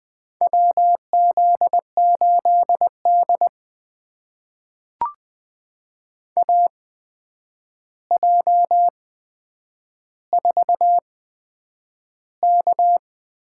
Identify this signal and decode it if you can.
Morse code, heard as a keyed tone, about 20 wpm, spaced as for 9 wpm, 700 Hz. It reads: WZ8D AJ4K
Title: USA Call Signs - 8x Character Spacing - 20wpm